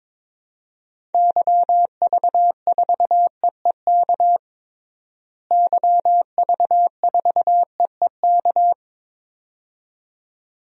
Morse code, heard as a keyed tone, about 22 wpm, 700 Hz. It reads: YV4EEK YV4EEK
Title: DXCC-100 Call Signs - 6 Characters - Repeated 2x - 22wpm